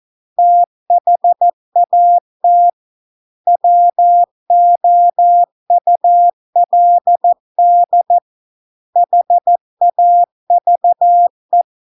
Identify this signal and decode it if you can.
Morse code, heard as a keyed tone, about 14 wpm, 700 Hz. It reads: THAT WOULD HAVE